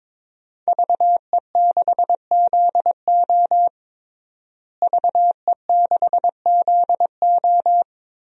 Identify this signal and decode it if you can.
Morse code, heard as a keyed tone, about 22 wpm, 700 Hz. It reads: VE6ZO VE6ZO
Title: DXCC-100 Call Signs - 5 Characters - Repeated 2x - 22wpm